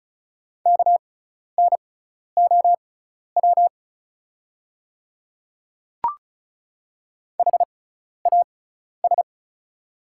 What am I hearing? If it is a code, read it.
Morse code, heard as a keyed tone, about 36 wpm, spaced as for 12 wpm, 690 Hz. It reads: KNOW HAS
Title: 6x Character Spacing - Top 200 Words - 35wpm